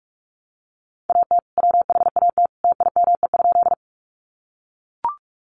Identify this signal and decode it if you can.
Morse code, heard as a keyed tone, about 44 wpm, 700 Hz. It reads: AT WHAT TIME?
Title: Sentences from Top 100 Words 45wpm